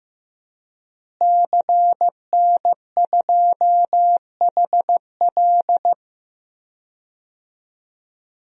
Morse code, heard as a keyed tone, about 15 wpm, 700 Hz. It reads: CN2HL
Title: DXCC-100 Call Signs - 5 Characters - Repeated 3x - 15wpm